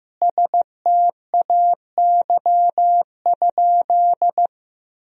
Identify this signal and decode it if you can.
Morse code, heard as a keyed tone, about 15 wpm, 700 Hz. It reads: STAY?
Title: Sentences from Top 1000 Words - Review Part 1 - 15wpm